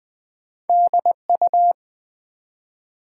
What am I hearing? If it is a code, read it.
Morse code, heard as a keyed tone, about 20 wpm, 700 Hz. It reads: DU
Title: Build A Word - 1 Letter At A Time - 3-Letter Words - 20wpm